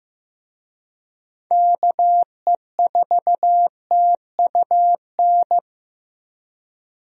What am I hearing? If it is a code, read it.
Morse code, heard as a keyed tone, about 15 wpm, 700 Hz. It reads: KE4TUN
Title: USA Call Signs - 6 Characters - 15wpm